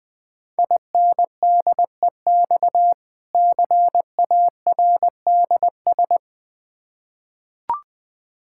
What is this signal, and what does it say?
Morse code, heard as a keyed tone, about 20 wpm, 690 Hz. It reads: INDEX CARDS
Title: Difficult Words with Hints - 20wpm